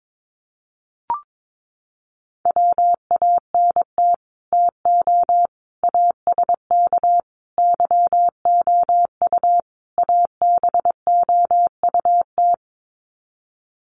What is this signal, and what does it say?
Morse code, heard as a keyed tone, about 22 wpm, 700 Hz. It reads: WANT TO ASK YOU ABOUT
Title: Sets of 5 Words Encore2 22wpm